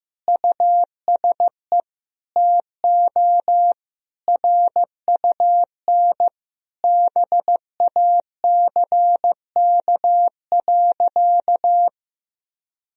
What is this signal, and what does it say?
Morse code, heard as a keyed tone, about 15 wpm, 700 Hz. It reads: USE TO RUN BACK.
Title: Sentences from Top 1000 Words - Review Part 1 - 15wpm